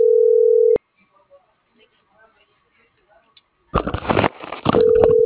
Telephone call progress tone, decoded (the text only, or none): ringback tone